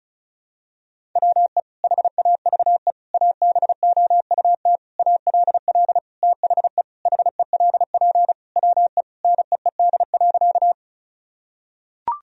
Morse code, heard as a keyed tone, about 36 wpm, 700 Hz. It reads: WE HAVE ABOUT ALL THE HELP WE NEED.
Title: Sentences from Top 1000 Words - Review Part 1 - 35wpm